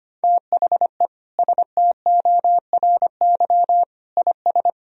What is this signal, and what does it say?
Morse code, heard as a keyed tone, about 25 wpm, 700 Hz. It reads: THE STORY IS
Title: Sentences from Top 1000 Words - Review Part 4 - 25wpm